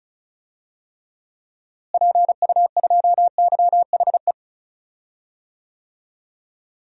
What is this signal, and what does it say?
Morse code, heard as a keyed tone, about 34 wpm, 700 Hz. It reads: PU2YHE
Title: DXCC-100 Call Signs - 6 Characters - Repeated 2x - 35wpm